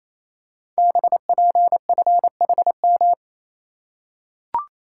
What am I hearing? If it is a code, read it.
Morse code, heard as a keyed tone, about 27 wpm, 700 Hz. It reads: BPFHM
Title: Code Groups - 5 Letters - 28wpm